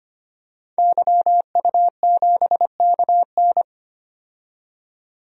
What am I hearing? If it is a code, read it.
Morse code, heard as a keyed tone, about 25 wpm, 700 Hz. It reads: YU7KN
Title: DXCC-100 Call Signs - 5 Characters - 25wpm